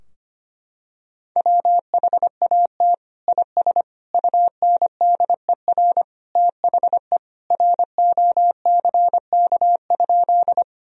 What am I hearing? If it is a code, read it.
Morse code, heard as a keyed tone, about 25 wpm, 700 Hz. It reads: WHAT IS UNDER THE ROCK?
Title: Sentences from Top 1000 Words - Review Part 2 - 25wpm